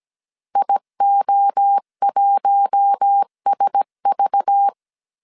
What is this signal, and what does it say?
Morse code, heard as a keyed tone, about 17 wpm, 800 Hz. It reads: IO1SV
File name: CW / DET / 1 6 1500C S C 16wpm 2733